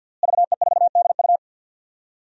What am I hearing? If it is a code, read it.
Morse code, heard as a keyed tone, about 51 wpm, 700 Hz. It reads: VE4DU
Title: DXCC-100 Call Signs - 5 Characters - Repeated 2x - 50wpm